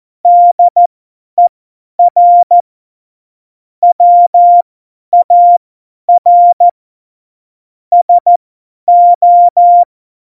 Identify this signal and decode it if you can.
Morse code, heard as a keyed tone, about 14 wpm, spaced as for 10 wpm, 700 Hz. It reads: DER WAR SO
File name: Morsekurs / Witze / Witz 07. Januar 2020, 50 F70